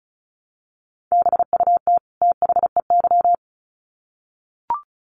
Morse code, heard as a keyed tone, about 34 wpm, 700 Hz. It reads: BUT THEY
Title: Sets of 2 Words 35wpm